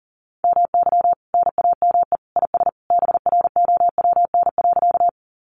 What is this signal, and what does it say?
Morse code, heard as a keyed tone, about 40 wpm, 700 Hz. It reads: MY NAME IS BROWN.
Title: Sentences from Top 1200 Words 40wpm